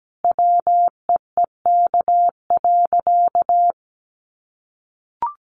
Morse code, heard as a keyed tone, about 17 wpm, 700 Hz. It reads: WEEK.
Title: Easy Phrases 17wpm